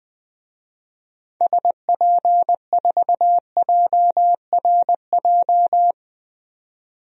Morse code, heard as a keyed tone, about 20 wpm, 700 Hz. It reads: SP4JRJ
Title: DXCC-100 Call Signs - 6 Characters - 20wpm